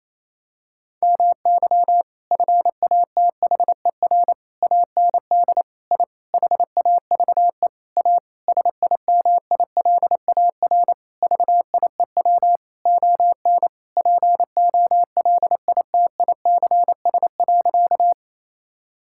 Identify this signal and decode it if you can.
Morse code, heard as a keyed tone, about 28 wpm, 700 Hz. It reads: MY FATHER AND I HAVE A SIMILAR VIEW ON POLITICS.
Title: Sentences from Top 1000 Words - Review Part 6 - 28wpm